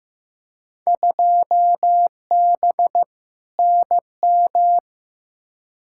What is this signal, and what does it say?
Morse code, heard as a keyed tone, about 15 wpm, 700 Hz. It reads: 2B NM
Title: Field Day Exchanges Encore - 15wpm